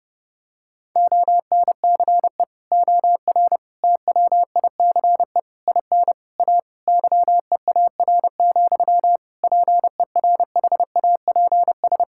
Morse code, heard as a keyed tone, about 32 wpm, 700 Hz. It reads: ONCE OR TWICE IN A YEAR, PERHAPS
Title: Sentences from Top 1600 Words 30wpm